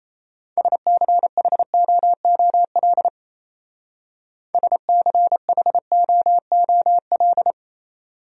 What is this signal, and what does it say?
Morse code, heard as a keyed tone, about 34 wpm, 700 Hz. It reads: SCHOOL SCHOOL
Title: Speed-Racing Top 500 Words 28wpm